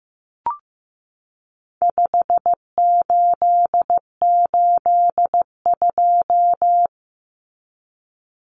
Morse code, heard as a keyed tone, about 15 wpm, 700 Hz. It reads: 5882
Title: Numbers - 4-Digit 15wpm